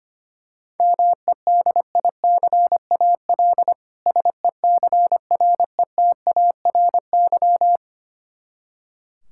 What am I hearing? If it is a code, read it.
Morse code, heard as a keyed tone, about 25 wpm, 700 Hz. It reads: MEDICAL SECRETARY
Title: Practice QSO - Exchange 3 - More Info - 25wpm